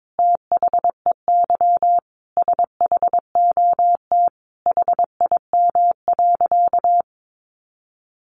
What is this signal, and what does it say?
Morse code, heard as a keyed tone, about 22 wpm, 700 Hz. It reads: THEY SHOT HIM.
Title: Sentences from Top 1100 Words 22wpm